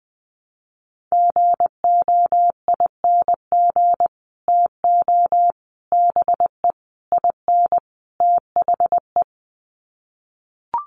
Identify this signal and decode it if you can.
Morse code, heard as a keyed tone, about 20 wpm, 700 Hz. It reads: GOING TO BE IN THE
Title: Sets of 5 Words Encore 20wpm